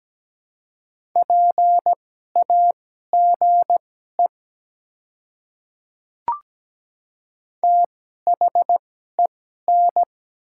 Morse code, heard as a keyed tone, about 17 wpm, spaced as for 12 wpm, 700 Hz. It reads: PAGE THEN
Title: 2x Character Spacing - Top 300 Words - 17wpm